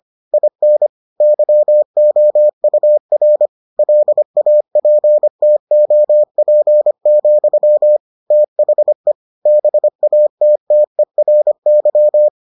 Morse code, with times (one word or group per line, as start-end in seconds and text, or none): 0.3-0.9 IN
1.2-3.5 YOUR
3.8-8.0 LAPTOP,
8.3-9.1 THE
9.5-12.3 BATTERY